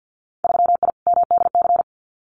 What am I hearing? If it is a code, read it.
Morse code, heard as a keyed tone, about 50 wpm, 700 Hz. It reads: 3I MDC
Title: Winter Field Day Exchanges Encore 50wpm